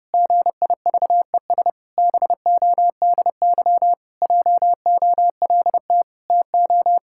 Morse code, heard as a keyed tone, about 30 wpm, 700 Hz. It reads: GIVES BODY JOLT TO
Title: Pangram Challenge v2 - 30wpm